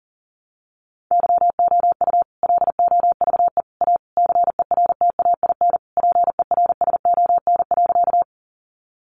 Morse code, heard as a keyed tone, about 40 wpm, 700 Hz. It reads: YOU LOVE A CERTAIN PERSON.
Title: Sentences from Top 500 Words 40wpm